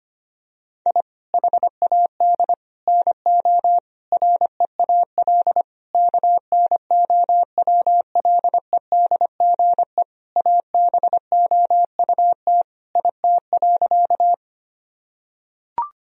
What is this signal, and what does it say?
Morse code, heard as a keyed tone, about 25 wpm, 700 Hz. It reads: I HAD NO REAL KNOWLEDGE ABOUT IT.
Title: Sentences from Top 1000 Words - Review Part 3 - 25wpm